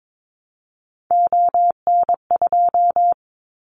Morse code, heard as a keyed tone, about 22 wpm, 700 Hz. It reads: ON2
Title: Callsign Prefixes for Top 50 DXCC - 22wpm